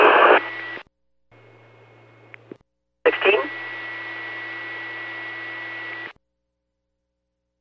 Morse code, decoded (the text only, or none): none